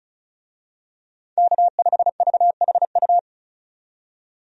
Morse code, heard as a keyed tone, about 36 wpm, 700 Hz. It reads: K5VHU
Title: USA Call Signs - 5 Characters - 35wpm